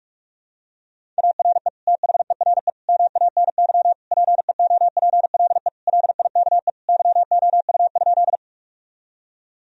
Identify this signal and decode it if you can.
Morse code, heard as a keyed tone, about 41 wpm, 700 Hz. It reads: ARE THERE MANY PEOPLE LIKE YOU?
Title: Sentences from Top 1000 Words - Review Part 1 - 45wpm